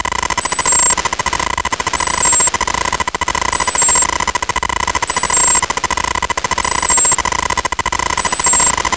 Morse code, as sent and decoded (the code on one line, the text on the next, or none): none